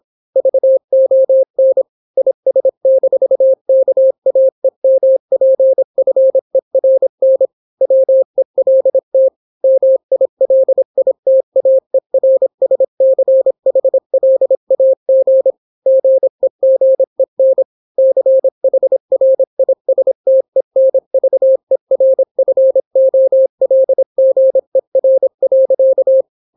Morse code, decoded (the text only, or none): VON IS-KAEMPFERN WELT MILITAERSCHLAG GEGEN CHRISTENVERFOLGER.